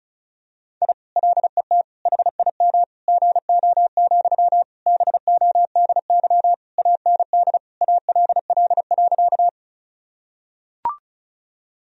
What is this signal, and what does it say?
Morse code, heard as a keyed tone, about 34 wpm, 700 Hz. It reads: I LET HIM GO, BODY AND ALL.
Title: Sentences from Top 1000 Words - Review Part 2 - 35wpm